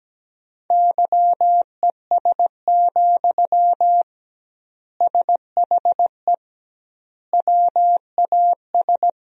Morse code, heard as a keyed tone, about 17 wpm, 700 Hz. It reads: YES, SHE WAS
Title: 2x Word-Spacing - Sentences from Top 500 Words - 17wpm